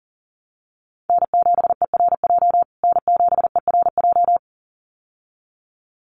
Morse code, heard as a keyed tone, about 40 wpm, 700 Hz. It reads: N7ERJ N7ERJ
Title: USA Call Signs Repeated 2x 40wpm